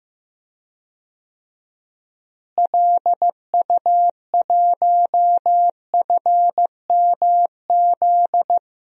Morse code, heard as a keyed tone, about 15 wpm, 700 Hz. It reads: LU1FMZ
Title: DXCC-100 Call Signs - 6 Characters - Repeated 3x - 15wpm